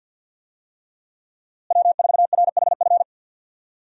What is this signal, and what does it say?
Morse code, heard as a keyed tone, about 52 wpm, 700 Hz. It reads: W4RHF
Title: USA Call Signs - 5 Characters - Repeated 3x - 50wpm